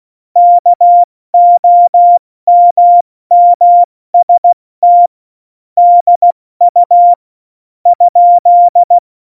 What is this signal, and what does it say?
Morse code, heard as a keyed tone, about 16 wpm, 700 Hz. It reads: KOMMST DU ?
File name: Morsekurs / Witze / Witz 07. Mai 2019, 70 F80